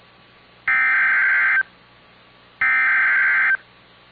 Emergency Alert System (SAME) tones (header bursts), heard as harsh buzzing bursts. It is heard from 0.7 to 3.5 s.